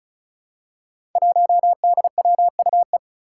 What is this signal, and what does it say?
Morse code, heard as a keyed tone, about 36 wpm, 700 Hz. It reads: 1DWUE
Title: Five Challenging Letter-Number Combinations 35wpm